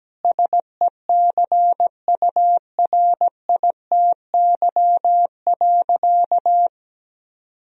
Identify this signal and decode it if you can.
Morse code, heard as a keyed tone, about 17 wpm, 700 Hz. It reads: SECURITY.
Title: Sentences from Top 1000 Words - Review Part 4 - 17wpm